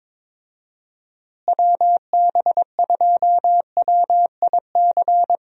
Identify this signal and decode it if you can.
Morse code, heard as a keyed tone, about 22 wpm, 700 Hz. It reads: WB2WIC
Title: USA Call Signs - 6 Characters - Repeated 3x - 22wpm